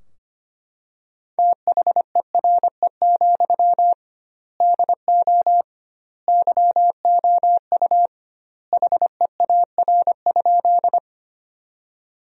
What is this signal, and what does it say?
Morse code, heard as a keyed tone, about 25 wpm, 700 Hz. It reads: THERE, DO YOU HEAR?
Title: 2x Word-Spacing - Sentences from Top 300 Words - 25wpm